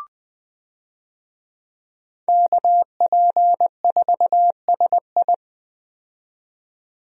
Morse code, heard as a keyed tone, about 20 wpm, 700 Hz. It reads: KP4SI
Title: DXCC-100 Call Signs - 5 Characters - 20wpm